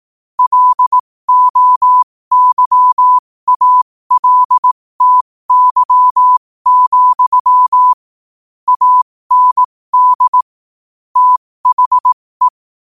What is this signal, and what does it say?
Morse code, heard as a keyed tone, about 18 wpm, spaced as for 15 wpm, 1000 Hz. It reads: LOYALTY, AND THE
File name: qotdfiles / 15wpm